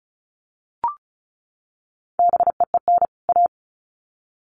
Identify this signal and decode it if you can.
Morse code, heard as a keyed tone, about 34 wpm, 700 Hz. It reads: BEEN A
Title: Sets of 2 Words 35wpm